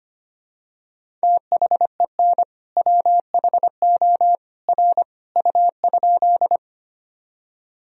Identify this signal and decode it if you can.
Morse code, heard as a keyed tone, about 25 wpm, 700 Hz. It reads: THEN WHO R U?
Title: Sentences from Top 100 Words and Abbreviations 25wpm